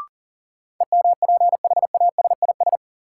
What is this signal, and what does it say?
Morse code, heard as a keyed tone, about 37 wpm, 700 Hz. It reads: EMPHASIS
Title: Sentences from Top 1400 Words - New Word Drill Alt - 40wpm